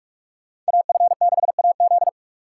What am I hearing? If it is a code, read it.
Morse code, heard as a keyed tone, about 59 wpm, spaced as for 51 wpm, 700 Hz. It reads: AF6AZ
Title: Speed Racing - USA 2x2 Call Signs 45wpm